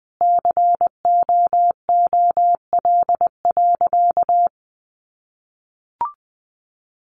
Morse code, heard as a keyed tone, about 20 wpm, 700 Hz. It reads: COOL.
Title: Sentences from Top 1200 Words 20wpm